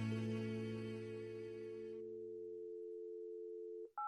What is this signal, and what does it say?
Dial tone (North America), 350 + 440 Hz.
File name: wrct-log hi- 08